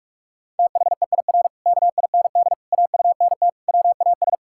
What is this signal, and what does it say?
Morse code, heard as a keyed tone, about 48 wpm, 700 Hz. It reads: THEIR KIND AUNT WAS
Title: Pangram Challenge v2 - 45wpm